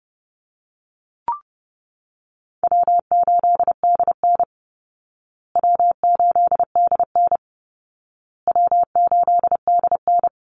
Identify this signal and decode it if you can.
Morse code, heard as a keyed tone, about 30 wpm, 700 Hz. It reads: W8DN W8DN W8DN
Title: Call Signs Repeated 2x - 30wpm